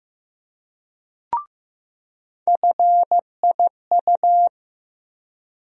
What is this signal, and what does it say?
Morse code, heard as a keyed tone, about 15 wpm, 700 Hz. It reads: FIU
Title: USA Call Sign Postfixes - 3 Letters 15wpm